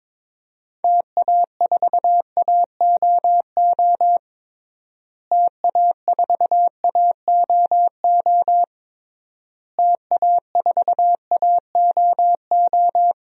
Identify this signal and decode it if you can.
Morse code, heard as a keyed tone, about 22 wpm, 700 Hz. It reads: TA4AOO TA4AOO TA4AOO
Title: DXCC-100 Call Signs - 6 Characters - Repeated 3x - 22wpm